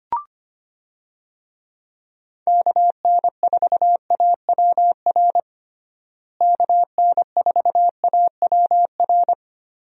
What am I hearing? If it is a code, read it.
Morse code, heard as a keyed tone, about 25 wpm, 700 Hz. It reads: KN4AWR KN4AWR
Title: USA Call Signs - 6 Characters - Repeated 2x - 25wpm